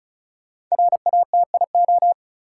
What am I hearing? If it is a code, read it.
Morse code, heard as a keyed tone, about 34 wpm, 700 Hz. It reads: RATIO
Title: Five Letter Words - 35wpm